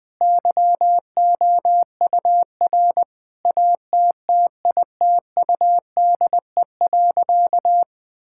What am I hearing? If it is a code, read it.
Morse code, heard as a keyed tone, about 20 wpm, 700 Hz. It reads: YOUR ATTITUDE.
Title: Sentences with 5-Word Phrases - 20wpm